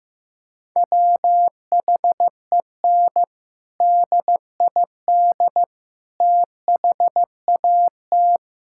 Morse code, heard as a keyed tone, about 16 wpm, 700 Hz. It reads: WHEN DID THAT
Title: Sentences from Top 1000 Words - Review Part 6 - 15wpm